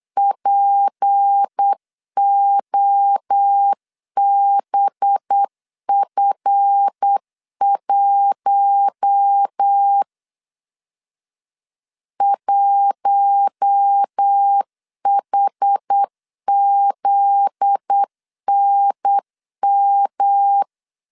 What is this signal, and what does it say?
Morse code, heard as a keyed tone, about 8 wpm, 800 Hz. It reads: POBF1 1HZNM
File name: LEZIONI / 7 GF9Z / 7 GF9Z 8wpm 3213